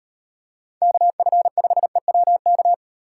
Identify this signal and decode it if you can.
Morse code, heard as a keyed tone, about 38 wpm, 700 Hz. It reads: KF5EWK